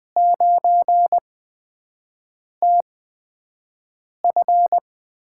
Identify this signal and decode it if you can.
Morse code, heard as a keyed tone, about 20 wpm, 700 Hz. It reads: 9 T F
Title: USA Call Signs - 5 Characters - 8x Character Spacing - 20wpm